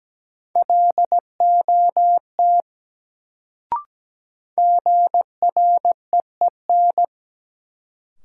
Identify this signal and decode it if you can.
Morse code, heard as a keyed tone, about 17 wpm, 700 Hz. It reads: LOT GREEN
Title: Sentences from Top 500 Words - New Word Drill - 17wpm